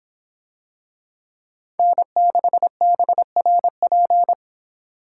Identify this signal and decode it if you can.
Morse code, heard as a keyed tone, about 25 wpm, 700 Hz. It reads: N6BRP